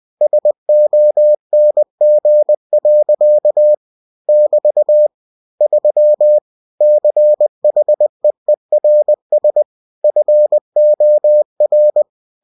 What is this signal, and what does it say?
Morse code, heard as a keyed tone, about 20 wpm, 600 Hz. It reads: SONG. <BT> 3 CHEERS FOR
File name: cw-books / winnie-the-pooh / winnie-20wpm-page-0103